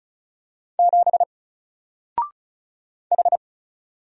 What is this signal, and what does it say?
Morse code, heard as a keyed tone, about 36 wpm, 700 Hz. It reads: 7 H
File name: Single-Letter-Number-35wpm